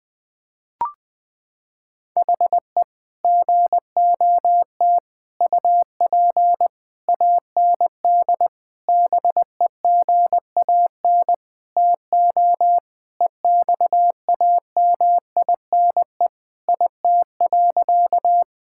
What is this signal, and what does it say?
Morse code, heard as a keyed tone, about 21 wpm, 700 Hz. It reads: HE GOT UP AND BEGAN TO EXAMINE IT.
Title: Sentences from Top 1600 Words 20wpm